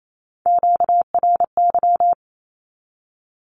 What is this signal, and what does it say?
Morse code, heard as a keyed tone, about 28 wpm, 700 Hz. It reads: QRY
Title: Q-Codes 28wpm